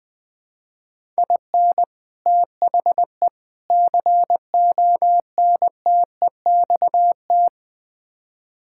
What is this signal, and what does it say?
Morse code, heard as a keyed tone, about 20 wpm, 700 Hz. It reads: IN THE CONTEXT
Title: Build A Phrase - 5-Word Phrases - 20wpm